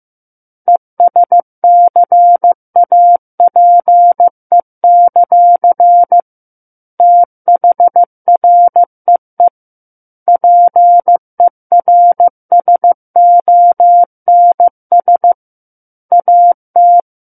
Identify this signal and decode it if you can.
Morse code, heard as a keyed tone, about 15 wpm, 700 Hz. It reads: ESCAPE; THREE PERSONS AT
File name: Chapter 6 - 15 WPM